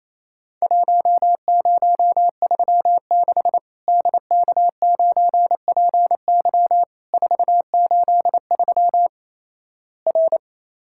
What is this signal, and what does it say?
Morse code, heard as a keyed tone, about 28 wpm, 700 Hz. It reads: 1036 DK9PY 483 R
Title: Worked All Europe DX Contest - QTC Exchange - 28wpm